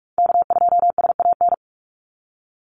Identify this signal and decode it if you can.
Morse code, heard as a keyed tone, about 46 wpm, 700 Hz. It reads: K2SAN
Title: Call Signs - 45wpm